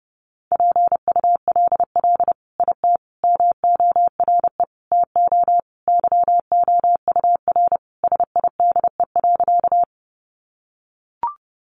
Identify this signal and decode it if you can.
Morse code, heard as a keyed tone, about 30 wpm, 700 Hz. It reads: PULL IT MORE TO YOUR SIDE.
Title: Sentences from Top 400 Words 30wpm